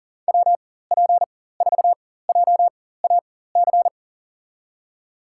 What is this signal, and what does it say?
Morse code, heard as a keyed tone, about 40 wpm, spaced as for 19 wpm, 700 Hz. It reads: WP4JAC
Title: USA Call Signs - 6 Characters - 4x Character Spacing - 40wpm